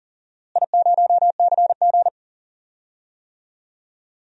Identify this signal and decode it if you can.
Morse code, heard as a keyed tone, about 39 wpm, 700 Hz. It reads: I0CG
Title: DXCC-100 Call Signs - 4 Characters - Repeated 2x - 40wpm